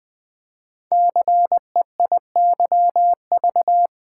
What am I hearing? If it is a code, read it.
Morse code, heard as a keyed tone, about 20 wpm, 700 Hz. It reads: CEIYV
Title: Code Groups - 5 Letter:Numbers - 20wpm